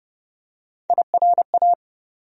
Morse code, heard as a keyed tone, about 30 wpm, 700 Hz. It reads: IRA
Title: USA Call Sign Postfixes - 3 Letters 30wpm